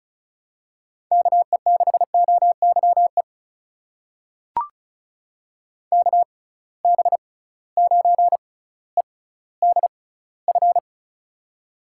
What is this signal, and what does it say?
Morse code, heard as a keyed tone, about 36 wpm, spaced as for 18 wpm, 700 Hz. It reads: KE6OYE KB9EDF
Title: USA Call Signs - 6 Characters - 6x Character Spacing - 35wpm